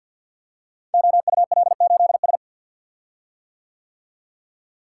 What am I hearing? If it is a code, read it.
Morse code, heard as a keyed tone, about 50 wpm, 700 Hz. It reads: OUL8S
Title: Code Groups - 5 Letter:Numbers - 50wpm